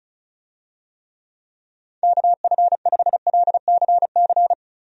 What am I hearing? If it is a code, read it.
Morse code, heard as a keyed tone, about 35 wpm, 700 Hz. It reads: KF5LCC
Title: USA Call Signs - 6 Characters - Repeated 2x - 35wpm